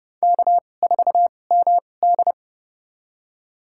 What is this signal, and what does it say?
Morse code, heard as a keyed tone, about 30 wpm, spaced as for 22 wpm, 700 Hz. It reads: K4MD
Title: USA Call Signs - 2x Character Spacing - 30wpm